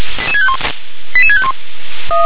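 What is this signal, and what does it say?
Five-tone selective call (ZVEI), heard as quick runs of five tones, tones of about 70 ms. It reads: 88651
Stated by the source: ZVEI five-tone selective call received on an SDR.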